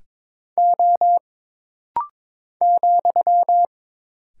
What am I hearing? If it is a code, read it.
Morse code, heard as a keyed tone, about 22 wpm, 700 Hz. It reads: O ,
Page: This rapid-fire timing is geared to push your ICR (Instant Character Recognition) skills.